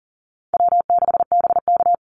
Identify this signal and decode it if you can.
Morse code, heard as a keyed tone, about 40 wpm, 700 Hz. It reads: W6BK
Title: Easy Call Signs - 40wpm